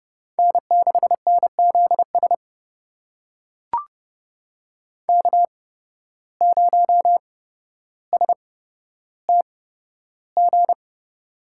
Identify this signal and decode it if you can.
Morse code, heard as a keyed tone, about 30 wpm, spaced as for 13 wpm, 700 Hz. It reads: N6NZS K0STG
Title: USA Call Signs - 5 Characters - 8x Character Spacing - 30wpm